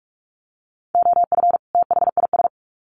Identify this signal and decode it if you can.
Morse code, heard as a keyed tone, about 44 wpm, 700 Hz. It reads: OF THIS
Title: Sets of 2 Words 45wpm